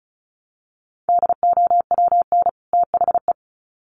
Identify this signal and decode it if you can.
Morse code, heard as a keyed tone, about 34 wpm, 700 Hz. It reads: DOWN THE
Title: Sets of 2 Words 35wpm